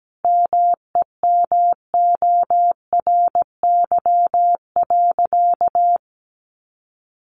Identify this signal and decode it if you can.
Morse code, heard as a keyed tone, about 17 wpm, 700 Hz. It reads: MEMORY.
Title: Sentences from Top 1300 Words 17wpm